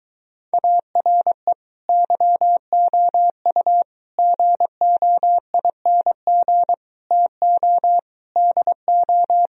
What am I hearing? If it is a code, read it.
Morse code, heard as a keyed tone, about 23 wpm, 700 Hz. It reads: ARE YOU GOING TO DO